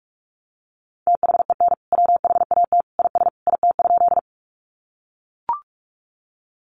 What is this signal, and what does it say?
Morse code, heard as a keyed tone, about 44 wpm, 700 Hz. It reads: THEN WHAT IS IT?
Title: Sentences from Top 100 Words 45wpm